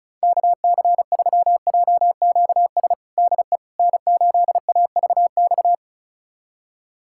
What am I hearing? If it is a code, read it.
Morse code, heard as a keyed tone, about 34 wpm, 700 Hz. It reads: KC3JQS DE N8AVX
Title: USA Callsign DE Callsign 35wpm